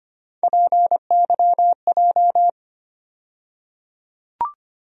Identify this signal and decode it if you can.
Morse code, heard as a keyed tone, about 25 wpm, 700 Hz. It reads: PYJ
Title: USA Call Sign Postfixes - 3 Letters 25wpm